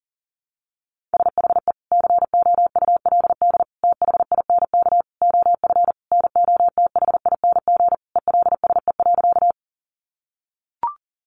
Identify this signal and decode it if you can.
Morse code, heard as a keyed tone, about 40 wpm, 700 Hz. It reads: SHE COULD THINK OF NOTHING ELSE.
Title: Sentences from Top 800 Words 40wpm